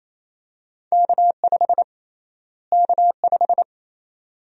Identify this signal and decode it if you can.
Morse code, heard as a keyed tone, about 28 wpm, 700 Hz. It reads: K5 K5
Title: USA Call Sign Prefixes - Repeated 2x - 28wpm